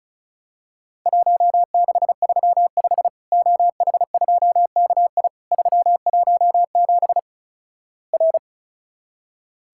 Morse code, heard as a keyed tone, about 36 wpm, 700 Hz. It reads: 1635 OH2KI 317 R
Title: Worked All Europe DX Contest - QTC Exchange - 35wpm